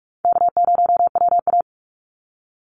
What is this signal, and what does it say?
Morse code, heard as a keyed tone, about 44 wpm, 700 Hz. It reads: K0WA
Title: Easy Call Signs Repeated 3x - 45wpm